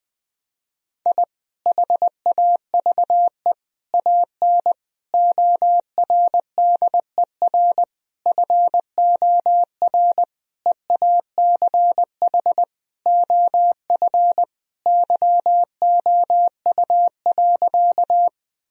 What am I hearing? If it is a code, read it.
Morse code, heard as a keyed tone, about 20 wpm, 700 Hz. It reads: I HAVE AN ORDER FOR EACH OF YOU.
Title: Sentences from Top 1000 Words - Review Part 2 - 20wpm